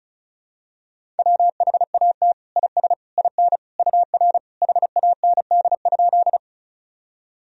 Morse code, heard as a keyed tone, about 36 wpm, 700 Hz. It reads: WHAT IS IN UR HAND?
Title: Sentences from Top 200 Words and Abbreviations 35wpm